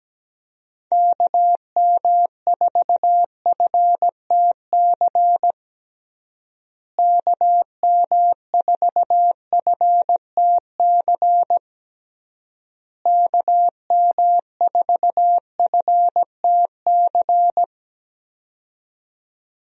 Morse code, heard as a keyed tone, about 17 wpm, 700 Hz. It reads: KM4FTC KM4FTC KM4FTC
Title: USA Call Signs - 6 Characters - Repeated 3x - 17wpm